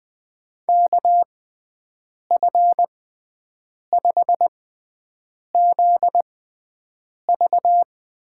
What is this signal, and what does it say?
Morse code, heard as a keyed tone, about 21 wpm, spaced as for 7 wpm, 700 Hz. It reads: KF5ZV
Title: USA Call Signs - 6 Characters - 6x Character Spacing - 20wpm